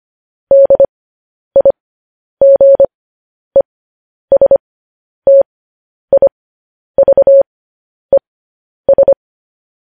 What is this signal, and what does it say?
Morse code, heard as a keyed tone, about 25 wpm, spaced as for 10 wpm, 550 Hz. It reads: DIGESTIVES
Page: news headlines in morse code at 10 wpm